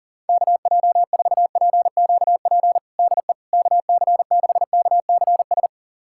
Morse code, heard as a keyed tone, about 40 wpm, 700 Hz. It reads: KJ4PQP DE KC6KCS
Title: USA Callsign DE Callsign - 6 Characters - 40wpm